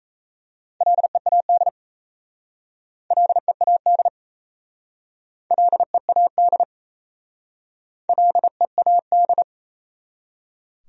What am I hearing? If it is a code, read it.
Morse code, heard as a keyed tone, about 37 wpm, 690 Hz. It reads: LEAD LEAD LEAD LEAD